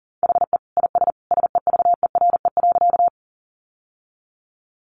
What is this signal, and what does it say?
Morse code, heard as a keyed tone, about 40 wpm, 700 Hz. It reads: HE IS SEVERE.
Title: Sentences from Top 1400 Words 40wpm